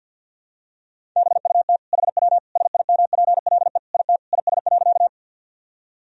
Morse code, heard as a keyed tone, about 52 wpm, 700 Hz. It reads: BUT HW SIMPLE IT IS.
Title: Sentences from Top 400 Words and Abbreviations 50wpm